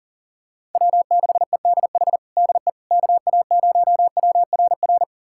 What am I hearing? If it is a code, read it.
Morse code, heard as a keyed tone, about 40 wpm, 700 Hz. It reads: W6EDH DE KA0WRR
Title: USA Callsign DE Callsign 40wpm